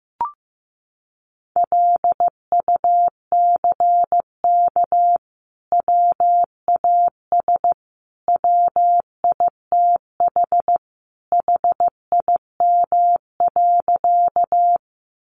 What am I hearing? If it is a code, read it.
Morse code, heard as a keyed tone, about 15 wpm, 700 Hz. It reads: LUCK WAS WITH HIM.
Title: Sentences from Top 1500 Words 15wpm